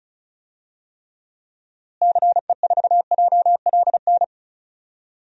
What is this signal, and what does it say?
Morse code, heard as a keyed tone, about 36 wpm, 700 Hz. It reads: CE4JLN
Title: DXCC-100 Call Signs - 6 Characters - Repeated 3x - 35wpm